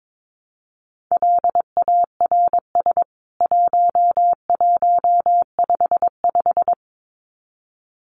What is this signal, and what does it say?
Morse code, heard as a keyed tone, about 22 wpm, 700 Hz. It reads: LARS 1155
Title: CWT Exchange 22wpm